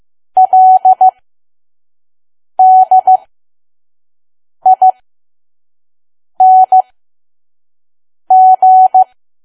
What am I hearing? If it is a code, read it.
Morse code, heard as a keyed tone, about 15 wpm, spaced as for 5 wpm, 750 Hz. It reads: LDING
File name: Archive / 5 WPM files / PM